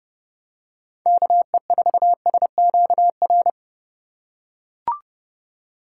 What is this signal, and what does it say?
Morse code, heard as a keyed tone, about 32 wpm, 700 Hz. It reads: KE4SQR
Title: USA Call Signs - 6 Characters - 4x Character Spacing - 30wpm